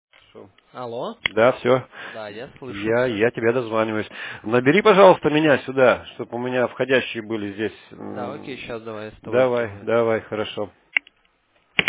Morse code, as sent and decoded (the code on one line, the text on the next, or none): none